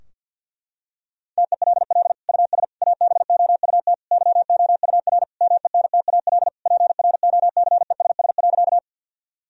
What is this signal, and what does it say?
Morse code, heard as a keyed tone, about 47 wpm, 700 Hz. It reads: TELL US ABOUT YOUR MENTAL PROCESS.
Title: Sentences from Top 1000 Words - Review Part 4 - 50wpm